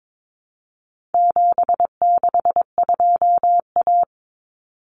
Morse code, heard as a keyed tone, about 22 wpm, 700 Hz. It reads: 762A
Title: Common Cut Numbers - 4-Digit 22wpm